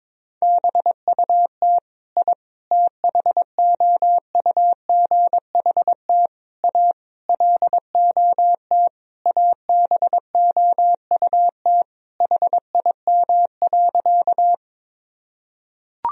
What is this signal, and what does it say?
Morse code, heard as a keyed tone, about 22 wpm, 700 Hz. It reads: BUT I THOUGHT A LOT ABOUT HIM.
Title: Sentences from Top 1000 Words - Review Part 3 - 22wpm